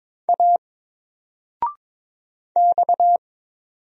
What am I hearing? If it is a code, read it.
Morse code, heard as a keyed tone, about 22 wpm, 700 Hz. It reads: A X
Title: Single Letter-Number Alt 22wpm